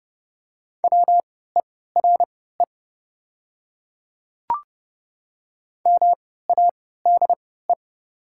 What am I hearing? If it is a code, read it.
Morse code, heard as a keyed tone, about 30 wpm, spaced as for 17 wpm, 690 Hz. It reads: WERE MADE
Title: 3x Character Spacing - Top 200 Words - 30wpm